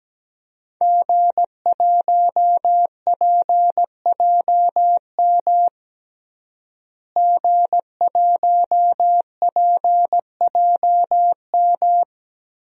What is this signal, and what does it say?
Morse code, heard as a keyed tone, about 17 wpm, 700 Hz. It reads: G1PJM G1PJM
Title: DXCC-100 Call Signs - 5 Characters - Repeated 3x - 17wpm